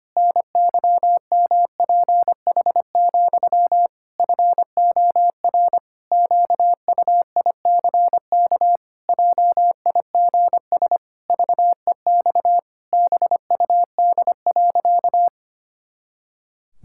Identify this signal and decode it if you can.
Morse code, heard as a keyed tone, about 25 wpm, 700 Hz. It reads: NYMPH, FOR QUICK JIGS VEX BUD.
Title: Pangram Challenge v2 - 25wpm